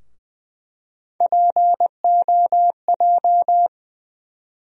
Morse code, heard as a keyed tone, about 20 wpm, 700 Hz. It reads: POJ
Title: USA Call Sign Postfixes - 3 Letters 20wpm